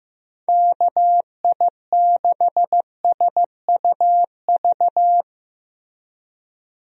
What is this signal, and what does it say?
Morse code, heard as a keyed tone, about 16 wpm, 700 Hz. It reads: KI6SUV
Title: USA Call Signs - 6 Characters - Repeated 2x - 15wpm